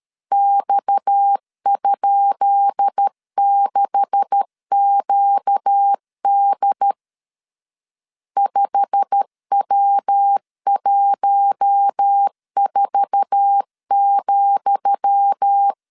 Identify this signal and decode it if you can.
Morse code, heard as a keyed tone, about 13 wpm, 800 Hz. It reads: X?6QD 5W14,
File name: LEZIONI / 10 SIGNS / 10D COMA 12wpm 2544